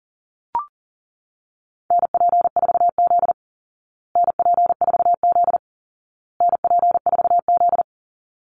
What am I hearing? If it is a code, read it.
Morse code, heard as a keyed tone, about 40 wpm, 700 Hz. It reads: NP4Z NP4Z NP4Z
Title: Call Signs Repeated 2x - 40wpm